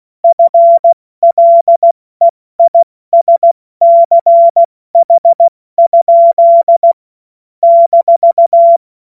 Morse code, heard as a keyed tone, about 16 wpm, spaced as for 14 wpm, 680 Hz. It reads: FLEISCH? -
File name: Morsekurs / Witze / Witz 04. Oktober 2022 70 F80